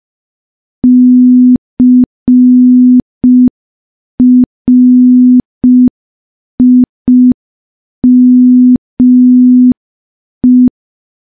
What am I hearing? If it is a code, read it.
Morse code, heard as a keyed tone, about 5 wpm, 250 Hz. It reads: CRIME
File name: morse / morse-training-05WPM